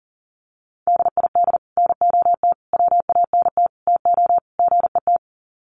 Morse code, heard as a keyed tone, about 40 wpm, 700 Hz. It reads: DID NOT WANT TO GET
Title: Sets of 5 Words Encore2 40wpm